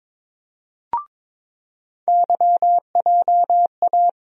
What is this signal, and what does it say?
Morse code, heard as a keyed tone, about 22 wpm, 700 Hz. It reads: YJA
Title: USA Call Sign Postfixes - 3 Letters 22wpm